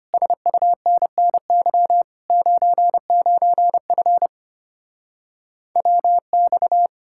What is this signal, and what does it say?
Morse code, heard as a keyed tone, about 33 wpm, 700 Hz. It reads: SUNNY 99F WX